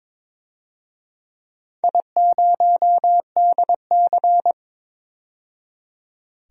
Morse code, heard as a keyed tone, about 22 wpm, 700 Hz. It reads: I0DC
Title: DXCC-100 Call Signs - 4 Characters - 22wpm